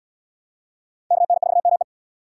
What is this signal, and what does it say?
Morse code, heard as a keyed tone, about 76 wpm, 700 Hz. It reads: KN4ME